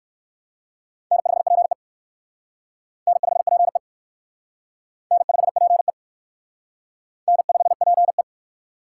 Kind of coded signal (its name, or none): Morse code